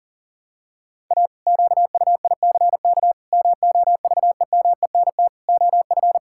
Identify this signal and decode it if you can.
Morse code, heard as a keyed tone, about 37 wpm, 700 Hz. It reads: A QUICK MOVEMENT OF
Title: Pangram Challenge v2 - 40wpm